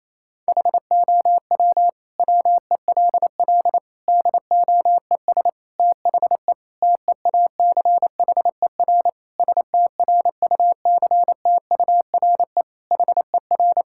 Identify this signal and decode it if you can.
Morse code, heard as a keyed tone, about 28 wpm, 700 Hz. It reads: HOW WELL DOES THE TEACHER STRUCTURE HER
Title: Sentences from Top 1000 Words - Review Part 6 - 28wpm